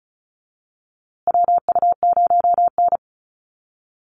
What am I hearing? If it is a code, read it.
Morse code, heard as a keyed tone, about 34 wpm, 700 Hz. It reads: WU0N
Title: USA Call Signs Repeated 2x 35wpm